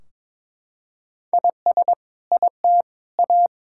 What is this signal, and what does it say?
Morse code, heard as a keyed tone, about 22 wpm, 700 Hz. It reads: IS IT A
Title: Sentences from Top 1600 Words 22wpm